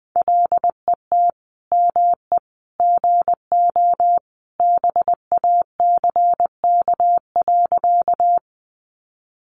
Morse code, heard as a keyed tone, about 20 wpm, 700 Hz. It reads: LET ME GO BACK.
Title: Sentences from Top 300 Words 20wpm